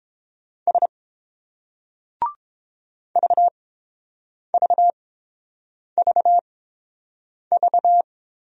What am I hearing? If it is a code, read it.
Morse code, heard as a keyed tone, about 30 wpm, spaced as for 7 wpm, 700 Hz. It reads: S VVVV